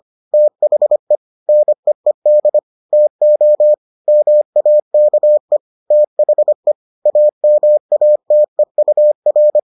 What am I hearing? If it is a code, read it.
Morse code, heard as a keyed tone, about 25 wpm, 600 Hz. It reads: THE NEED TO MAKE THE AMATEUR